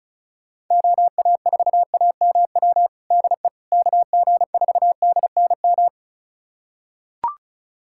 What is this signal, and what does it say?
Morse code, heard as a keyed tone, about 36 wpm, 700 Hz. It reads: OA4AMW DE KG4DNM
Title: USA Callsign DE Callsign 35wpm